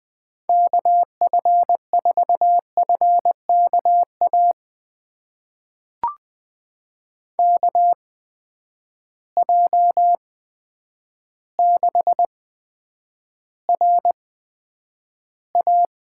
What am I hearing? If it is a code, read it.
Morse code, heard as a keyed tone, about 20 wpm, spaced as for 9 wpm, 700 Hz. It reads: KF4FKA KJ6RA
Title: USA Call Signs - 6 Characters - 8x Character Spacing - 20wpm